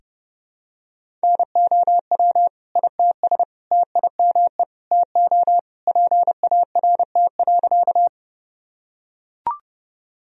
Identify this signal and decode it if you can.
Morse code, heard as a keyed tone, about 30 wpm, 700 Hz. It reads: NOW ITS TIME TO PART.
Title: Sentences from Top 1000 Words - Review Part 1 - 30wpm